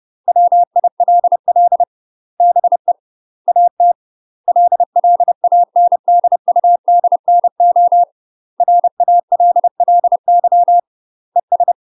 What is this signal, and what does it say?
Morse code, heard as a keyed tone, about 30 wpm, 700 Hz. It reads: WILL BE AT LLANDUDNO RALLY ES
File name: UK MASTER QSOs TRAINING / 30WPM / UK MASTER QSOs 0018